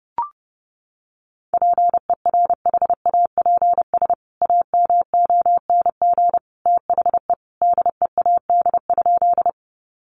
Morse code, heard as a keyed tone, about 30 wpm, 700 Hz. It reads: PERHAPS AMONG THE DEAD?
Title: Sentences from Top 1500 Words 30wpm